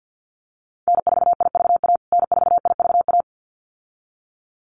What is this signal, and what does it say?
Morse code, heard as a keyed tone, about 50 wpm, 700 Hz. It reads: N4IVA N4IVA
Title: USA Call Signs Repeated 2x 50wpm